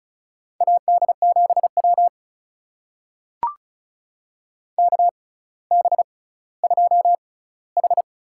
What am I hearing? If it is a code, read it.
Morse code, heard as a keyed tone, about 36 wpm, spaced as for 18 wpm, 700 Hz. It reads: AD7W KB2H
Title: USA Call Signs - 6x Character Spacing - 35wpm